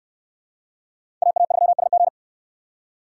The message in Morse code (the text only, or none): AA3SG